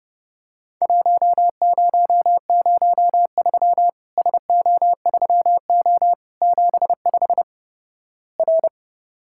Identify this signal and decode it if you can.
Morse code, heard as a keyed tone, about 30 wpm, 700 Hz. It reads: 1003 SO3O 75 R
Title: Worked All Europe DX Contest - QTC Exchange - 30wpm